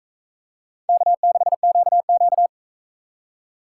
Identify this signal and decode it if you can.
Morse code, heard as a keyed tone, about 40 wpm, 700 Hz. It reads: K6QQ